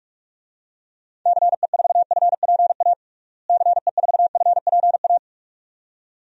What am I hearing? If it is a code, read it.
Morse code, heard as a keyed tone, about 46 wpm, 700 Hz. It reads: CE4FPA CE4FPA
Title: DXCC-100 Call Signs - 6 Characters - Repeated 2x - 45wpm